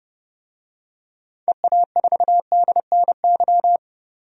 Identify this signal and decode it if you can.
Morse code, heard as a keyed tone, about 30 wpm, 700 Hz. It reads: EA4DNY
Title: DXCC-100 Call Signs - 6 Characters - 30wpm